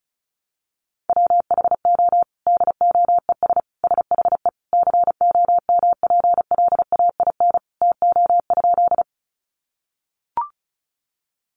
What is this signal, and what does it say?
Morse code, heard as a keyed tone, about 34 wpm, 700 Hz. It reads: WHO DOES SHE COMPLAIN TO?
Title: Sentences from Top 1300 Words 35wpm